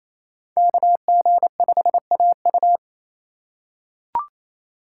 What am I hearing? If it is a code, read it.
Morse code, heard as a keyed tone, about 28 wpm, 700 Hz. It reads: KG5AU
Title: USA Call Signs - 5 Characters - 2x Character Spacing - 28wpm